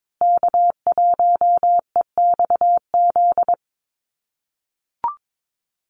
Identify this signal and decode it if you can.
Morse code, heard as a keyed tone, about 22 wpm, 700 Hz. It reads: K1EXZ
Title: USA Call Signs Repeated 2x 22wpm